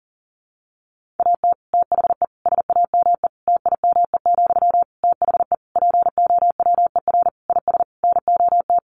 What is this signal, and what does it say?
Morse code, heard as a keyed tone, about 40 wpm, 700 Hz. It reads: AT THE SAME TIME, THE POWER IS NOT
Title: Sentences from Top 1500 Words 40wpm